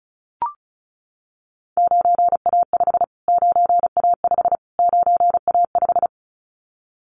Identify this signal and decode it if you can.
Morse code, heard as a keyed tone, about 34 wpm, 700 Hz. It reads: 9A5 9A5 9A5
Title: Callsign Prefixes for Top 25 DXCC - Repeated 3x - 35wpm